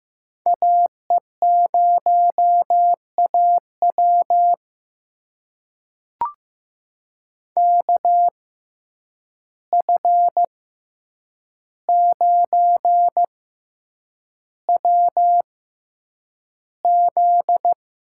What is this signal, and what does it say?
Morse code, heard as a keyed tone, about 15 wpm, spaced as for 8 wpm, 700 Hz. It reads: AE0AW KF9WZ
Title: USA Call Signs - 5 Characters - 6x Character Spacing - 15wpm